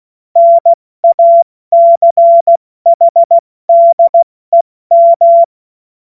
Morse code, heard as a keyed tone, about 16 wpm, 680 Hz. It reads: NACHDEM
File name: Morsekurs / Witze / Witz 14.November 2023 70 F80